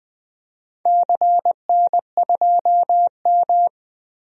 Morse code, heard as a keyed tone, about 20 wpm, 700 Hz. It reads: CN2M
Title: DXCC-100 Call Signs - 4 Characters - 20wpm